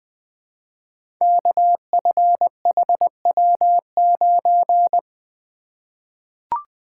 Morse code, heard as a keyed tone, about 20 wpm, 700 Hz. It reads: KFHW9
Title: Code Groups - 5 Letter:Numbers - 20wpm